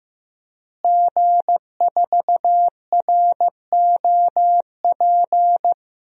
Morse code, heard as a keyed tone, about 16 wpm, 700 Hz. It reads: G4ROP
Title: DXCC-100 Call Signs - 5 Characters - Repeated 2x - 15wpm